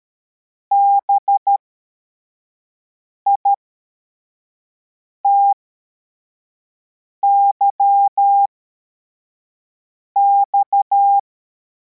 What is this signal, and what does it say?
Morse code, heard as a keyed tone, about 13 wpm, spaced as for 4 wpm, 800 Hz. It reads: BITYX
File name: formation / cw / Text64